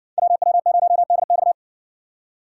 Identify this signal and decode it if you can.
Morse code, heard as a keyed tone, about 60 wpm, 700 Hz. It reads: WW0DX